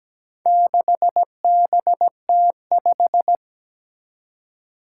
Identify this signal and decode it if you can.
Morse code, heard as a keyed tone, about 17 wpm, 700 Hz. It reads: 6BT5
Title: Four Challenging Letter-Number Combinations 17wpm